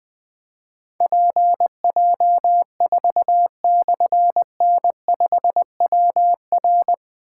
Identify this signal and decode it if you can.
Morse code, heard as a keyed tone, about 20 wpm, 700 Hz. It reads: PJ4/N5WR
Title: Difficult Call Signs 20wpm